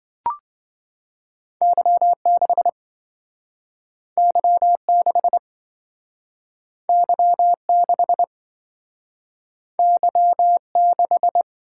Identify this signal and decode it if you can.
Morse code, heard as a keyed tone, about 27 wpm, 700 Hz. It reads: Y6 Y6 Y6 Y6